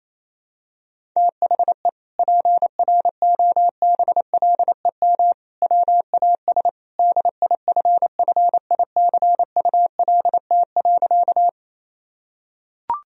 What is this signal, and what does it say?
Morse code, heard as a keyed tone, about 28 wpm, 700 Hz. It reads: THE PROBLEM WAS DIFFICULT.
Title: Sentences from Top 1000 Words - Review Part 2 - 28wpm